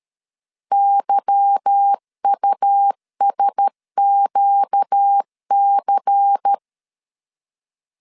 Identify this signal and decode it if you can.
Morse code, heard as a keyed tone, about 13 wpm, 800 Hz. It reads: YUSQC
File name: LEZIONI / 8 YJ45 / 8 Y 12wpm 2228